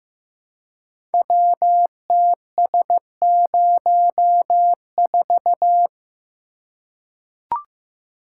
Morse code, heard as a keyed tone, about 15 wpm, 700 Hz. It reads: WTS04
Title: Code Groups - 5 Letter:Numbers - 15wpm